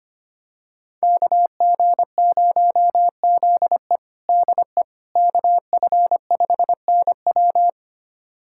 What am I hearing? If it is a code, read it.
Morse code, heard as a keyed tone, about 25 wpm, 700 Hz. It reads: KG0ZE DE KF5NW
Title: USA Callsign DE Callsign - 5 Characters - 25wpm